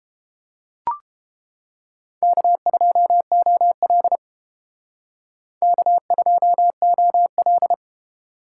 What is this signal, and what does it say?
Morse code, heard as a keyed tone, about 29 wpm, 700 Hz. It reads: K2OL K2OL